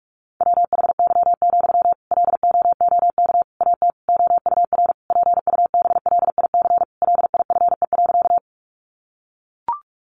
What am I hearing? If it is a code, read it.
Morse code, heard as a keyed tone, about 44 wpm, 700 Hz. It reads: WHY, LOOK AT OUR PUBLIC LIFE.
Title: Sentences from Top 600 Words 45wpm